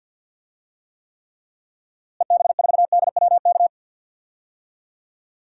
Morse code, heard as a keyed tone, about 51 wpm, 700 Hz. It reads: EB4DWK
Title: DXCC-100 Call Signs - 6 Characters - Repeated 3x - 50wpm